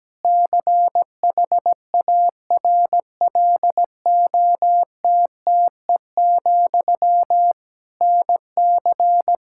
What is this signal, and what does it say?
Morse code, heard as a keyed tone, about 17 wpm, 700 Hz. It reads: CHARLOTTE, NC
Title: Practice QSO - Exchange 1 - Beginning - 17wpm